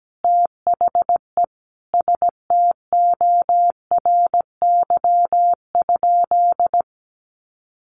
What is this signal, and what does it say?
Morse code, heard as a keyed tone, about 17 wpm, 700 Hz. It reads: THE STORY?
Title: Sentences from Top 1200 Words 17wpm